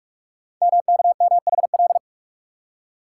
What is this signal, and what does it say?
Morse code, heard as a keyed tone, about 44 wpm, 700 Hz. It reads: MKMHL
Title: Code Groups - 5 Letters - 45wpm